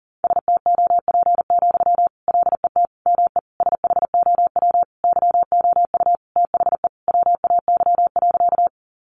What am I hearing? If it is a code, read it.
Morse code, heard as a keyed tone, about 40 wpm, 700 Hz. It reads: STOP, LET ME SHOW YOU THE WAY.
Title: Sentences from Top 300 Words 40wpm